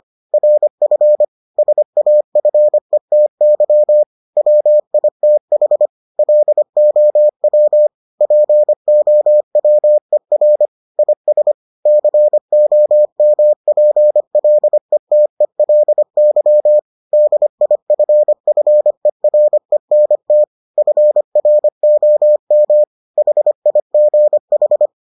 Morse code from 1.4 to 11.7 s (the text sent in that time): SAFETY WITH LOW POWER IS